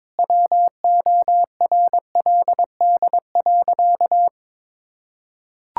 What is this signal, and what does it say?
Morse code, heard as a keyed tone, about 22 wpm, 700 Hz. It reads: WORLD.
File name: practice / 22wpm / Sentences with 3-Word Phrases - 22wpm